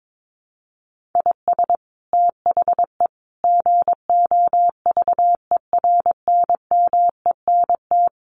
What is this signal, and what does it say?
Morse code, heard as a keyed tone, about 22 wpm, 700 Hz. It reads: IS THE GOVERNMENT
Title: Sentences from Top 1500 Words 22wpm